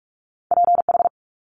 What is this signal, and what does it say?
Morse code, heard as a keyed tone, about 46 wpm, 700 Hz. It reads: PH
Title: Multi-letter Phonemes 45wpm